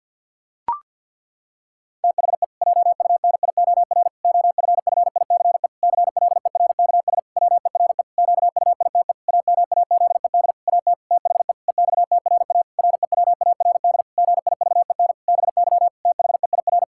Morse code, heard as a keyed tone, about 49 wpm, 700 Hz. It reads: THE JUNIOR OFFICE CLERKS WERE QUITE AMAZED AT THE EXTRA REWARD GIVEN BY THEIR
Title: Pangram Challenge v2 - 50wpm